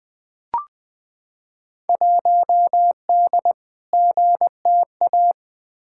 Morse code, heard as a keyed tone, about 20 wpm, 700 Hz. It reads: 1D GTA
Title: Field Day Exchanges - 20wpm